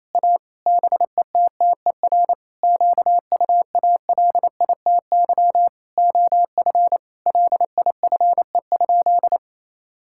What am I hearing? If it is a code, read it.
Morse code, heard as a keyed tone, about 28 wpm, 700 Hz. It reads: A BETTER QUALITY OF LIFE?
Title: Sentences from Top 1000 Words - Review Part 5 - 28wpm